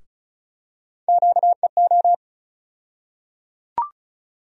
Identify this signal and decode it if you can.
Morse code, heard as a keyed tone, about 36 wpm, 700 Hz. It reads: QEO
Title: USA Call Sign Postfixes - 3 Letters 35wpm